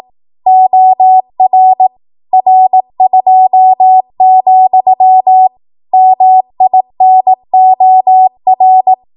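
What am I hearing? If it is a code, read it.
Morse code, heard as a keyed tone, about 18 wpm, 750 Hz. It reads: OR R2, MINOR
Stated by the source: Code practice at 18 WPM, a transition run bridging 15 to 20 WPM.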